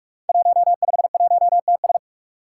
Morse code, heard as a keyed tone, about 44 wpm, 700 Hz. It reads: 151TS
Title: Five Challenging Letter-Number Combinations 45wpm